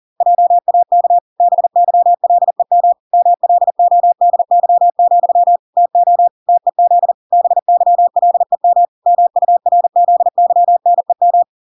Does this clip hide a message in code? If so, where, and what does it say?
Yes: Morse code from 0.2 to 11.4 s, heard as a keyed tone, about 40 wpm, 700 Hz. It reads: JAK BYLEM MLODY, TO TEZ BYLEM MURZYNEM